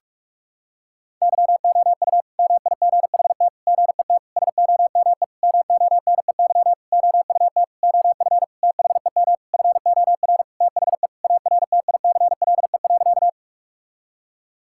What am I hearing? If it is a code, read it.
Morse code, heard as a keyed tone, about 46 wpm, 700 Hz. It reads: YOU MIGHT GET SOME MONEY OUT OF THEM FOR THE ARTICLE.
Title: Sentences from Top 1000 Words - Review Part 5 - 45wpm